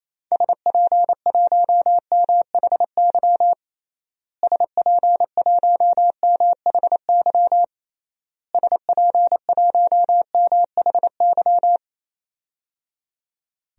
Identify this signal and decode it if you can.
Morse code, heard as a keyed tone, about 28 wpm, 700 Hz. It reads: SP1MHY SP1MHY SP1MHY
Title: DXCC-100 Call Signs - 6 Characters - Repeated 3x - 28wpm